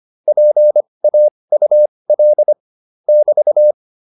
Morse code, heard as a keyed tone, about 25 wpm, spaced as for 20 wpm, 600 Hz. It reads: PAUL <BT>